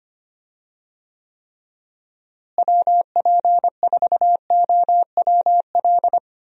Morse code, heard as a keyed tone, about 25 wpm, 700 Hz. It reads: WP4OWL
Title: USA Call Signs - 6 Characters - Repeated 2x - 25wpm